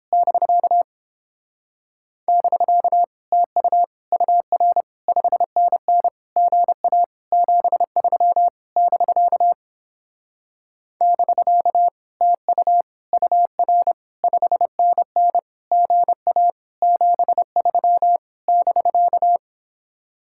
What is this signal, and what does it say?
Morse code, heard as a keyed tone, about 29 wpm, 700 Hz. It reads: <BK> <BK> TU UR 5NN GA 73 <BK> <BK> TU UR 5NN GA 73 <BK>